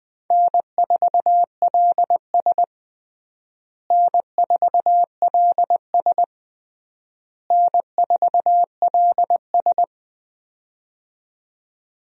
Morse code, heard as a keyed tone, about 21 wpm, 700 Hz. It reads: N4LS N4LS N4LS
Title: USA Call Signs - 4 Characters - Repeated 3x - 20wpm